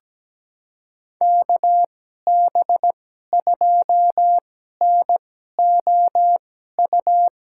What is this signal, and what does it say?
Morse code, heard as a keyed tone, about 17 wpm, spaced as for 12 wpm, 700 Hz. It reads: KB2NOU
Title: USA Call Signs - 6 Characters - 2x Character Spacing - 17wpm